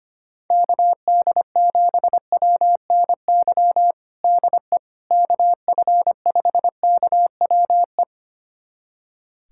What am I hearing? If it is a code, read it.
Morse code, heard as a keyed tone, about 25 wpm, 700 Hz. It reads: KD7WNY DE KF5KWE
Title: USA Callsign DE Callsign - 6 Characters - 25wpm